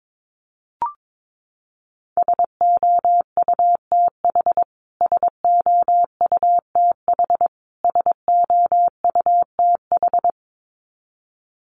Morse code, heard as a keyed tone, about 22 wpm, 700 Hz. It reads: SOUTH SOUTH SOUTH
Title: Top 500 Words Repeated 22wpm